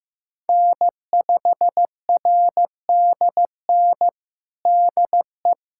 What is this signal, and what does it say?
Morse code, heard as a keyed tone, about 15 wpm, 700 Hz. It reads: N5RDN DE
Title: USA Callsign DE Callsign - 5 Characters - 15wpm